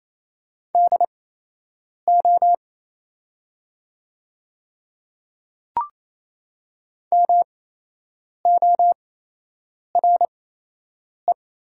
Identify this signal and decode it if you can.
Morse code, heard as a keyed tone, about 28 wpm, spaced as for 8 wpm, 700 Hz. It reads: DO MORE
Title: 8x Character Spacing - Top 100 Words - 28wpm